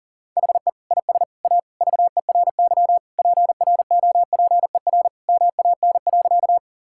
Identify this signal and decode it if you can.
Morse code, heard as a keyed tone, about 40 wpm, 700 Hz. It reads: HE IS A VERY PROPER MAN.
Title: Sentences from Top 1000 Words - Review Part 5 - 40wpm